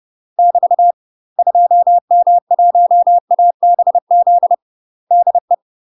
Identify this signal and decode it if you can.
Morse code, heard as a keyed tone, about 30 wpm, 700 Hz. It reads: <BT> 2M1ABZ DE
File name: UK MASTER QSOs TRAINING / 30WPM / UK MASTER QSOs 0003